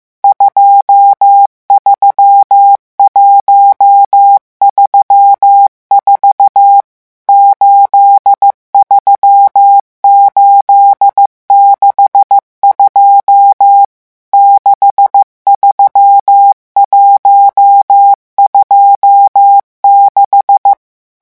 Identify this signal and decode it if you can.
Morse code, heard as a keyed tone, about 15 wpm, 800 Hz. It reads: 23134 83862 63126
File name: morse mp3 / revision11 chiffres vitesse 14